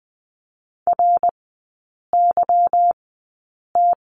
Morse code, heard as a keyed tone, about 20 wpm, spaced as for 8 wpm, 690 Hz. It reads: RYT
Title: 172 Lesson 28e - Number 9 - Three Characters 20wpm